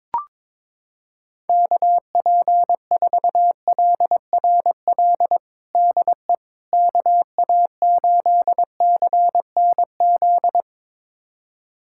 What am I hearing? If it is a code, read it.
Morse code, heard as a keyed tone, about 22 wpm, 700 Hz. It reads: KP4LRL DE KA8CNZ
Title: USA Callsign DE Callsign 22wpm